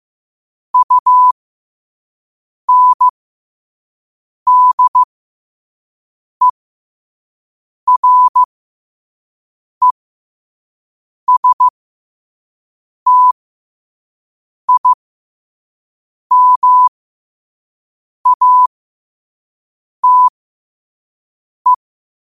Morse code, heard as a keyed tone, about 15 wpm, spaced as for 5 wpm, 1000 Hz. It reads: UNDERESTIMATE
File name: qotdfiles / 5wpm